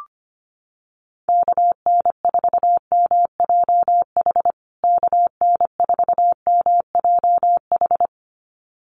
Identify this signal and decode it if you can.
Morse code, heard as a keyed tone, about 25 wpm, 700 Hz. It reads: KN4MJH KN4MJH
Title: USA Call Signs Repeated 2x 25wpm